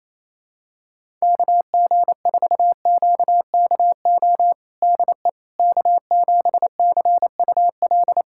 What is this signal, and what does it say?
Morse code, heard as a keyed tone, about 28 wpm, 700 Hz. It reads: KG4QKO DE K7CUL
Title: USA Callsign DE Callsign 28wpm